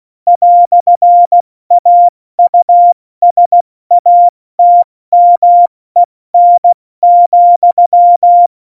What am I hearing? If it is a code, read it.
Morse code, heard as a keyed tone, about 16 wpm, spaced as for 14 wpm, 700 Hz. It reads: "AUSATMEN,
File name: Morsekurs / Witze / Witz 08. Dezember 2020, 70 F80